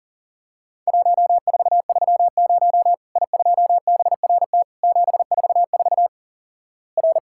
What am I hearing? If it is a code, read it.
Morse code, heard as a keyed tone, about 43 wpm, 700 Hz. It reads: 1430 I2BRT 744 R
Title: Worked All Europe DX Contest - QTC Exchange - 40wpm